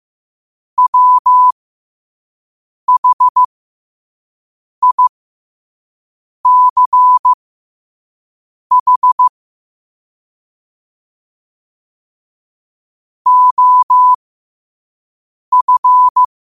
Morse code, heard as a keyed tone, about 15 wpm, spaced as for 5 wpm, 1000 Hz. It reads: WHICH OF
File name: qotdfiles / 5wpm